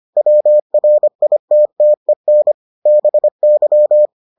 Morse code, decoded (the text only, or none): WRITTEN BY